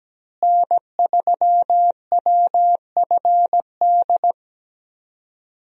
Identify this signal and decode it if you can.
Morse code, heard as a keyed tone, about 17 wpm, 700 Hz. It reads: N3WFD
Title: USA Call Signs - 5 Characters - Repeated 2x - 17wpm